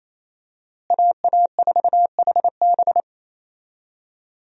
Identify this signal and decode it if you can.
Morse code, heard as a keyed tone, about 28 wpm, 700 Hz. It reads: AA4HB
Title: USA Call Signs - 5 Characters - Repeated 2x - 28wpm